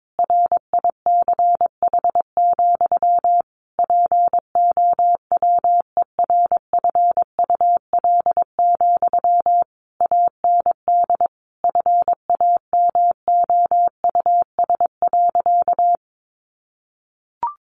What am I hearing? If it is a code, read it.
Morse code, heard as a keyed tone, about 22 wpm, 700 Hz. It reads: RICH, POWERFUL, AND FAMOUS.
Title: Sentences from Top 1500 Words 22wpm